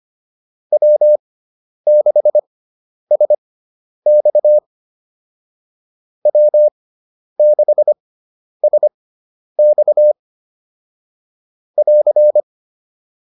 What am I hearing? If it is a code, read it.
Morse code, heard as a keyed tone, about 25 wpm, spaced as for 10 wpm, 600 Hz. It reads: W6SX W6SX <AR>
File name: QSO102 10